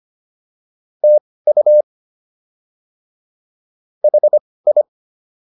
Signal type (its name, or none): Morse code